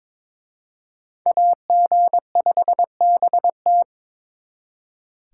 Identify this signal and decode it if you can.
Morse code, heard as a keyed tone, about 26 wpm, 700 Hz. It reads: AG5BT
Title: Speed Racing - USA Call Signs 22wpm